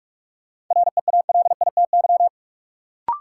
Morse code, heard as a keyed tone, about 43 wpm, 700 Hz. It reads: REALITY
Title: Sentences from Top 1200 Words - New Word Drill Alt - 45wpm